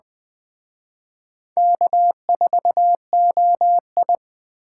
Morse code, heard as a keyed tone, about 20 wpm, 700 Hz. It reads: K4OI